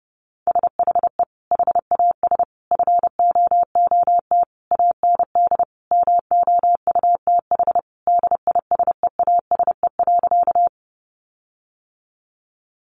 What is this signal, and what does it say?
Morse code, heard as a keyed tone, about 28 wpm, 700 Hz. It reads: SHE HAS FOOT AND MOUTH DISEASE.
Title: Sentences from Top 1500 Words 30wpm